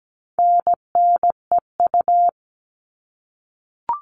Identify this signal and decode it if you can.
Morse code, heard as a keyed tone, about 17 wpm, 700 Hz. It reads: NNEU
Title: All Cut Numbers - 4-Digit 17wpm